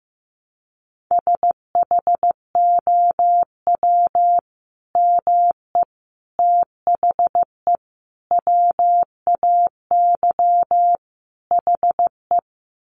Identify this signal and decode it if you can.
Morse code, heard as a keyed tone, about 15 wpm, 700 Hz. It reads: SHOW ME THE WAY HE
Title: Sentences from Top 200 Words 15wpm